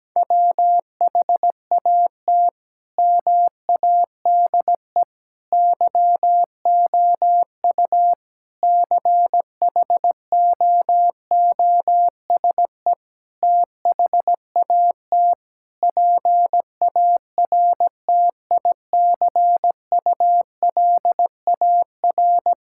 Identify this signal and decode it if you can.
Morse code, heard as a keyed tone, about 17 wpm, 700 Hz. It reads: WHAT MADE YOU CHOOSE THAT PARTICULAR
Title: Sentences from Top 1000 Words - Review Part 5 - 17wpm